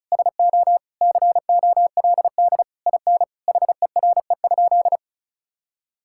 Morse code, heard as a keyed tone, about 34 wpm, 700 Hz. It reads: SO COLD IN HERE?
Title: Sentences from Top 1000 Words - Review Part 6 - 35wpm